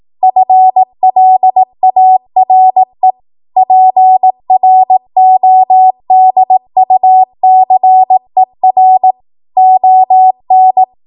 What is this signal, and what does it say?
Morse code, heard as a keyed tone, about 18 wpm, 750 Hz. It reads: FLARE PRODUCER ON
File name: Morse / 18WPM